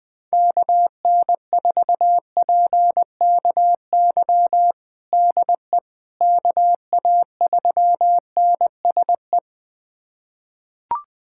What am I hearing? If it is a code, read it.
Morse code, heard as a keyed tone, about 20 wpm, 700 Hz. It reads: KN4PKY DE KA3NSE
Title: USA Callsign DE Callsign - 6 Characters - 20wpm